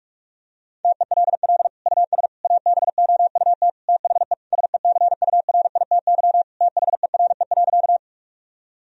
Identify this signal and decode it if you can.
Morse code, heard as a keyed tone, about 48 wpm, 700 Hz. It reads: TELL US ABOUT THE SECURITY THERE.
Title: Sentences from Top 1000 Words - Review Part 4 - 45wpm